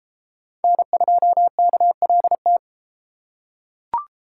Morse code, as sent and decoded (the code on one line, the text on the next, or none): -. ..--- -.- .-.. -
N2KLT